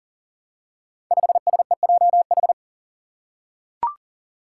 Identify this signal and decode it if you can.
Morse code, heard as a keyed tone, about 37 wpm, 700 Hz. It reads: 5SEJH
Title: Five Challenging Letter-Number Combinations 40wpm